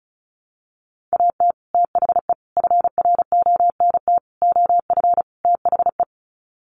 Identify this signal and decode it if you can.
Morse code, heard as a keyed tone, about 34 wpm, 700 Hz. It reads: AT THE FRONT OF THE
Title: Sets of 5 Words 35wpm